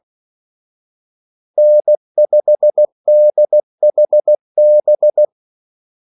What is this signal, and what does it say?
Morse code, heard as a keyed tone, about 16 wpm, 600 Hz. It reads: N5DHB
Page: Kaldesignaler 16 wpm